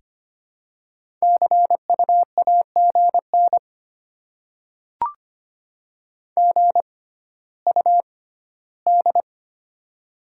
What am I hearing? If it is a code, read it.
Morse code, heard as a keyed tone, about 25 wpm, spaced as for 14 wpm, 700 Hz. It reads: CU AGN GUD
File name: Top 20 QSO Elements - 6x Character Spacing - 25wpm